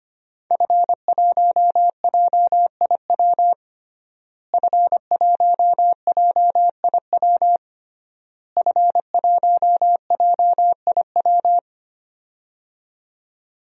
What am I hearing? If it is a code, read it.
Morse code, heard as a keyed tone, about 25 wpm, 700 Hz. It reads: F1JIW F1JIW F1JIW
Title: DXCC-100 Call Signs - 5 Characters - Repeated 3x - 25wpm